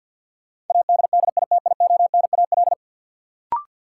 Morse code, heard as a keyed tone, about 48 wpm, 700 Hz. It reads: ADDITIONAL
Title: Sentences from Top 1400 Words - New Word Drill Alt - 50wpm